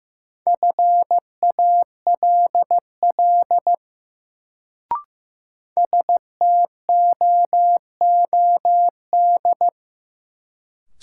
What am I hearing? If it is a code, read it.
Morse code, heard as a keyed tone, about 15 wpm, 700 Hz. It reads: FALL STOOD
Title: Sentences from Top 500 Words - New Word Drill - 15wpm